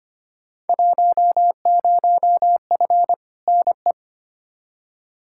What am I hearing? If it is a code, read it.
Morse code, heard as a keyed tone, about 25 wpm, 700 Hz. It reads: 10F NE
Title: Field Day Exchanges Encore - 25wpm